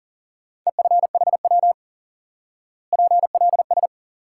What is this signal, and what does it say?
Morse code, heard as a keyed tone, about 40 wpm, 700 Hz. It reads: EFHW PLS
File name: IQE Course 3 - Lesson 89 - YAGI - Two Elements - 40wpm